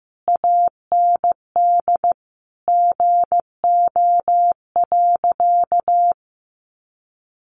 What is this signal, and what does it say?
Morse code, heard as a keyed tone, about 15 wpm, 700 Hz. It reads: AND GO.
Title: Sentences from Top 1200 Words 15wpm